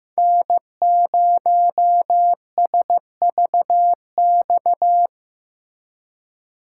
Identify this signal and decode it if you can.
Morse code, heard as a keyed tone, about 18 wpm, 700 Hz. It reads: N0SVX
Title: Speed Racing - USA 1x3 Call Signs 15wpm